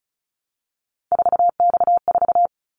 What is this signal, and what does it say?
Morse code, heard as a keyed tone, about 36 wpm, 690 Hz. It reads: 4X4
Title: Callsign Prefixes for Top 50 DXCC - 35wpm